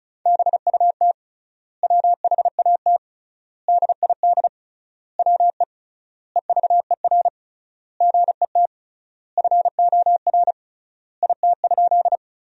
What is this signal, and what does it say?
Morse code, heard as a keyed tone, about 36 wpm, 700 Hz. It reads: BUT WHAT DID WE EVER GET FOR IT?
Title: 3x Word-Spacing - Sentences from Top 400 Words - 35wpm